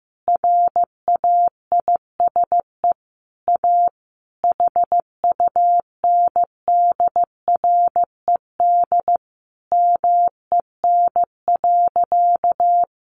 Morse code, heard as a keyed tone, about 14 wpm, 700 Hz. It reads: RAISE A HUNDRED MEN.
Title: Sentences from Top 1400 Words 15wpm